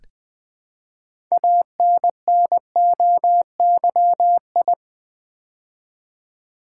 Morse code, heard as a keyed tone, about 20 wpm, 700 Hz. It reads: ANNOYI
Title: Build A Word - 3 Letters At A Time - 8-Letter Words - 20wpm